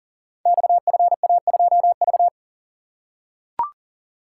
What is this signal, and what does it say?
Morse code, heard as a keyed tone, about 39 wpm, 700 Hz. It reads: XFA2V
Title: Code Groups - 5 Letter:Numbers - 40wpm